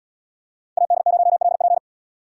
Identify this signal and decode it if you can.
Morse code, heard as a keyed tone, about 76 wpm, 700 Hz. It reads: AD0LP